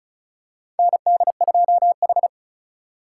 Morse code, heard as a keyed tone, about 36 wpm, 700 Hz. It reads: ND2H
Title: USA Call Signs - 4 Characters - 4x Character Spacing - 35wpm